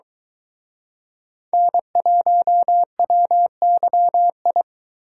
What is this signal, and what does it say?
Morse code, heard as a keyed tone, about 23 wpm, 700 Hz. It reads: N1WYI